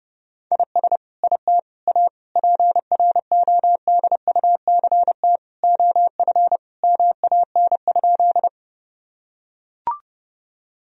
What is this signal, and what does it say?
Morse code, heard as a keyed tone, about 32 wpm, 700 Hz. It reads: IS IT A PRODUCT OF MAN?
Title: Sentences from Top 1000 Words - Review Part 2 - 30wpm